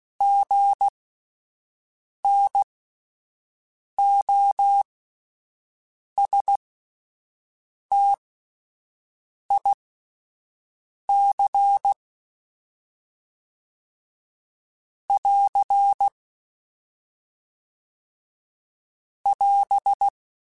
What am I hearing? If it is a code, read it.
Morse code, heard as a keyed tone, about 16 wpm, spaced as for 5 wpm, 800 Hz. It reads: GNOSTIC <AR> <AS>
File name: Spaced Alphas / Spaced Alphas mp3s / Gs sp @10